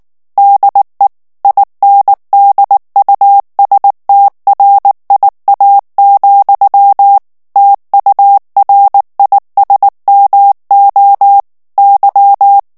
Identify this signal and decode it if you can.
Morse code, heard as a keyed tone, about 19 wpm, 800 Hz. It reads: DE INDUSTRIA, TURISMO Y
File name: morse mp3 / ESPAGNOL TEXTE EN CLAIR 2 VITESSE 18 MOTS MINUTES